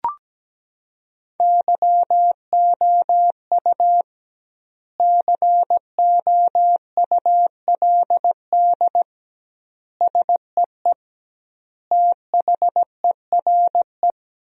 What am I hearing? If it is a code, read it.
Morse code, heard as a keyed tone, about 17 wpm, 700 Hz. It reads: YOU COULD SEE THERE
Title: 2x Word-Spacing - Sentences from Top 500 Words - 17wpm